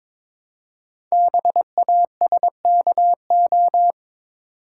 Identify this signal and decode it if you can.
Morse code, heard as a keyed tone, about 23 wpm, 700 Hz. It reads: BASKO
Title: Code Groups - 5 Letters - 22wpm